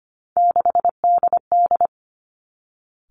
Morse code, heard as a keyed tone, about 25 wpm, 700 Hz. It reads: 6DD
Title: Three Challenging Letter-Number Combinations 25wpm